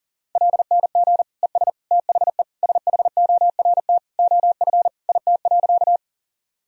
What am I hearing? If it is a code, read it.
Morse code, heard as a keyed tone, about 40 wpm, 700 Hz. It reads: LNG ES THE SHORT OF IT.
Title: Sentences from Top 400 Words and Abbreviations 40wpm